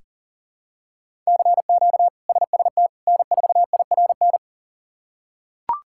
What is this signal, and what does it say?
Morse code, heard as a keyed tone, about 40 wpm, 700 Hz. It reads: CQ SST N4IRN
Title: K1USN SST QSO Practice - 40wpm